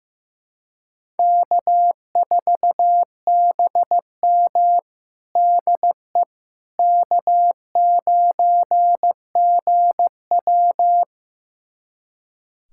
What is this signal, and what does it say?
Morse code, heard as a keyed tone, about 16 wpm, 700 Hz. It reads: K4BM DE K9GW
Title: USA Callsign DE Callsign - 4 Characters - 15wpm